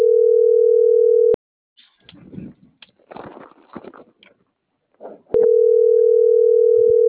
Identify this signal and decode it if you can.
Ringback tone (North America), 440 + 480 Hz.